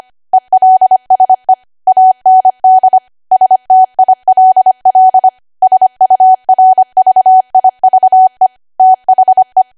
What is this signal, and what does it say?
Morse code, heard as a keyed tone, about 25 wpm, 750 Hz. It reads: ELSE AND STILL SURVIVE THE
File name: Archive / 25 WPM / 25WPM